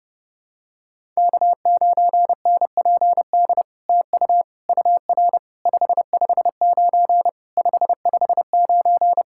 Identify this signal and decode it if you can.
Morse code, heard as a keyed tone, about 30 wpm, 700 Hz. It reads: K9NPD TU UR 559 559
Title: SOTA QSO Practice 30wpm